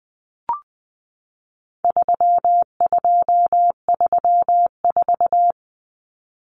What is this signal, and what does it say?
Morse code, heard as a keyed tone, about 20 wpm, 700 Hz. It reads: 3234
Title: Common Cut Numbers - 4-Digit 20wpm